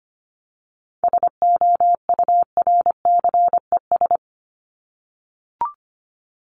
Morse code, heard as a keyed tone, about 25 wpm, 700 Hz. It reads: SOURCES
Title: ES:S Suffix - Top 100 Words with Base Words 25wpm